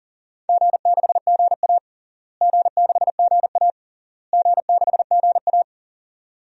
Morse code, heard as a keyed tone, about 37 wpm, 700 Hz. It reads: G6GA G6GA G6GA
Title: DXCC-100 Call Signs - 4 Characters - Repeated 3x - 40wpm